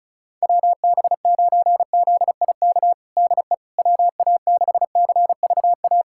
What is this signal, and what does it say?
Morse code, heard as a keyed tone, about 34 wpm, 700 Hz. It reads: WB9ZIK DE WA6CVA
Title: USA Callsign DE Callsign - 6 Characters - 35wpm